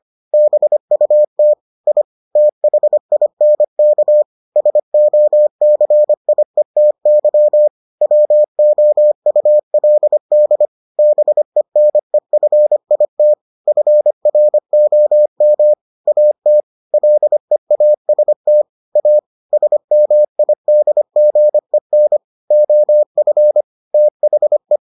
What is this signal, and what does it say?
Morse code, heard as a keyed tone, about 25 wpm, 600 Hz. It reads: BUT I THINK SOCIETY WOULD BENEFIT FROM AT LEAST A SMIDGEN OF THE